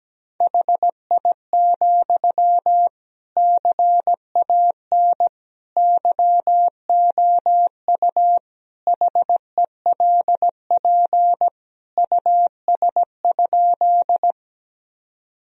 Morse code, heard as a keyed tone, about 17 wpm, 700 Hz. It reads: HI, CAN YOU HELP US?
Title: Sentences from Top 1000 Words - Review Part 3 - 17wpm